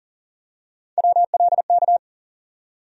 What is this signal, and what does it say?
Morse code, heard as a keyed tone, about 40 wpm, 700 Hz. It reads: WLK
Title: USA Call Sign Postfixes - 3 Letters 40wpm